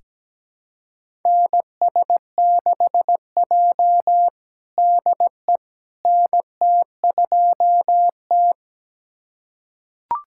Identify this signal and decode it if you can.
Morse code, heard as a keyed tone, about 17 wpm, 700 Hz. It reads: NS6J DE NT2T
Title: USA Callsign DE Callsign - 4 Characters - 17wpm